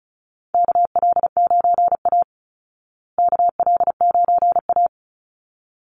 Morse code, heard as a keyed tone, about 34 wpm, 700 Hz. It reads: KL9A KL9A
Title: Call Signs Repeated 3x - 35wpm